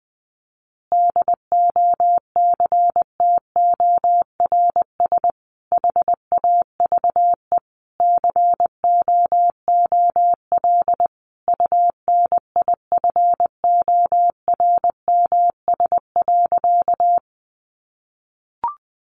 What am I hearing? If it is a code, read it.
Morse code, heard as a keyed tone, about 20 wpm, 700 Hz. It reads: DOCTORS HAVE COOL UNIFORMS.
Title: Easy Phrases 20wpm